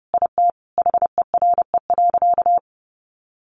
Morse code, heard as a keyed tone, about 30 wpm, 700 Hz. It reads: IT HERE.
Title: Sentences from Top 1100 Words 30wpm